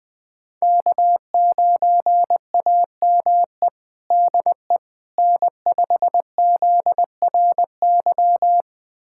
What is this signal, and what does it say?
Morse code, heard as a keyed tone, about 20 wpm, 700 Hz. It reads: K9AME DE N5ZRY
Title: USA Callsign DE Callsign - 5 Characters - 20wpm